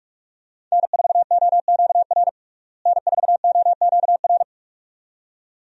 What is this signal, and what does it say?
Morse code, heard as a keyed tone, about 45 wpm, 700 Hz. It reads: N4OQR N4OQR
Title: USA Call Signs - 5 Characters - Repeated 2x - 45wpm